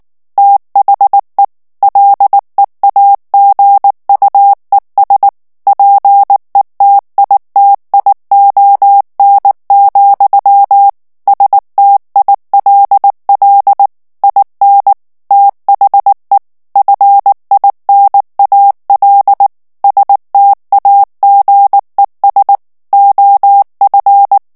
Morse code, heard as a keyed tone, about 19 wpm, 800 Hz. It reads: THE LEAGUES PETITION, STILL IN THE FINAL STAGES OF
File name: ANGLAIS texte clair 3 vitesse 18 mots minute